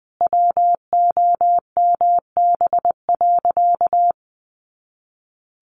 Morse code, heard as a keyed tone, about 20 wpm, 700 Hz. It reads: WOMB.
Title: A Sound Of Thunder 20wpm